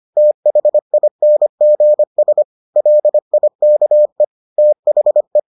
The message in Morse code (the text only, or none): THINGS LIKE THE